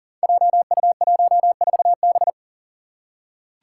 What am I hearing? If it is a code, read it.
Morse code, heard as a keyed tone, about 38 wpm, 700 Hz. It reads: JU14B
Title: Five Challenging Letter-Number Combinations 40wpm